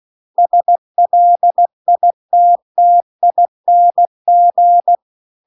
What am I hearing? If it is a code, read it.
Morse code, heard as a keyed tone, about 16 wpm, 700 Hz. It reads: SLITTING